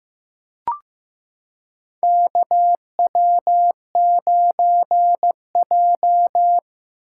Morse code, heard as a keyed tone, about 15 wpm, 700 Hz. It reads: KW9J
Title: USA Callsign DE Callsign - 4 Characters - 15wpm